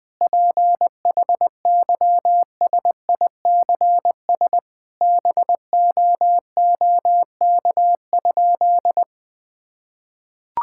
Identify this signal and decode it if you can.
Morse code, heard as a keyed tone, about 20 wpm, 700 Hz. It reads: PHYSICS BOOK?
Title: Sentences from Top 1000 Words - Review Part 5 - 20wpm